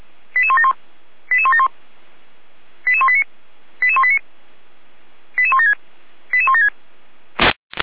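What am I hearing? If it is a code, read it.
Five-tone selective call (ZVEI), heard as quick runs of five tones, tones of about 80 ms. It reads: 88171 88171 88189 88189 88176 88176
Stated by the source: Five-tone ZVEI selective call received by SDR.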